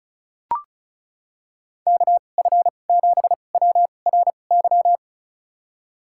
Morse code, heard as a keyed tone, about 34 wpm, spaced as for 25 wpm, 700 Hz. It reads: KF7WRY
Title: USA Call Signs - 2x Character Spacing - 35wpm